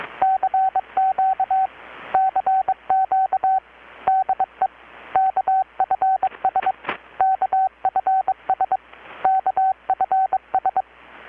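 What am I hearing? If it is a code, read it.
Morse code, heard as a keyed tone, about 21 wpm, 760 Hz, with radio noise. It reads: CQ CQ DE KFS KFS KFS
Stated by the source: Morse from KPH's Night of Nights XX maritime radiotelegraphy event, received in Chiapas, Mexico about 3,800 km away on a Kenwood R-600 with a grounded 10 m long-wire antenna.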